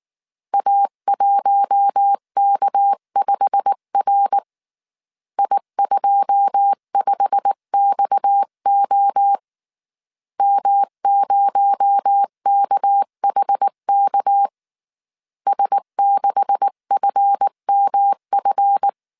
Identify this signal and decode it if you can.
Morse code, heard as a keyed tone, about 19 wpm, 800 Hz. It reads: A1K5R I25XO M0KHK S6FMF
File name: CW / DET / 1 9 1500C S C 18wpm 2539